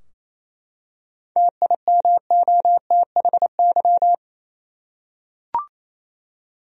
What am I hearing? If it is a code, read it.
Morse code, heard as a keyed tone, about 28 wpm, 700 Hz. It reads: TIMOTHY
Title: USA Common Names - 28wpm